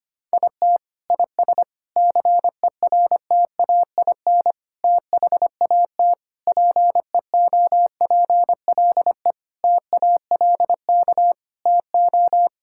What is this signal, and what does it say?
Morse code, heard as a keyed tone, about 25 wpm, 700 Hz. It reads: IT IS CERTAIN THAT PEOPLE TALK TO
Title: Sentences from Top 1000 Words - Review Part 5 - 25wpm